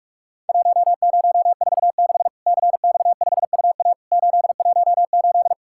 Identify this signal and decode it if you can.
Morse code, heard as a keyed tone, about 46 wpm, 700 Hz. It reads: 1046 CX5UA 818
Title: Worked All Europe DX Contest - QTC Exchange - 45wpm